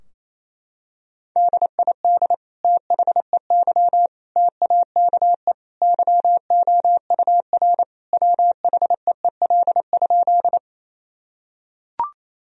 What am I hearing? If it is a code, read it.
Morse code, heard as a keyed tone, about 27 wpm, 700 Hz. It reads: DID THEY TAKE YOUR WHEEL?
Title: Sentences from Top 1000 Words - Review Part 3 - 28wpm